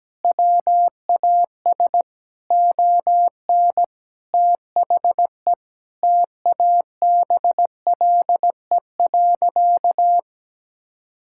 Morse code, heard as a keyed tone, about 17 wpm, 700 Hz. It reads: WAS ON THE TABLE.
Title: Sentences from Top 1000 Words - Review Part 6 - 17wpm